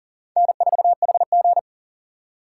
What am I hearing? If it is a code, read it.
Morse code, heard as a keyed tone, about 40 wpm, 700 Hz. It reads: N4HG
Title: USA Call Signs - 4 Characters - 4x Character Spacing - 40wpm